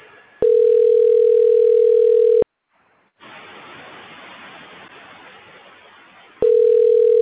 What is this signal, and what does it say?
Ringback tone (North America), 440 + 480 Hz.